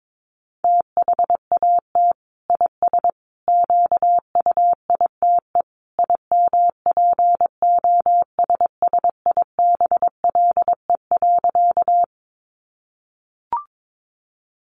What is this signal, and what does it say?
Morse code, heard as a keyed tone, about 22 wpm, 700 Hz. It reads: THAT IS QUITE IMPOSSIBLE.
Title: Sentences from Top 1100 Words 22wpm